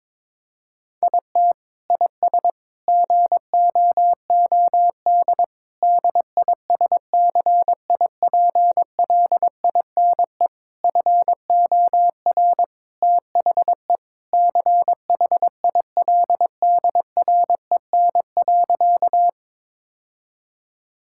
Morse code, heard as a keyed tone, about 22 wpm, 700 Hz. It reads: IT IS GOOD DISCIPLINE FOR THE CHILDREN.
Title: Sentences from Top 1000 Words - Review Part 6 - 22wpm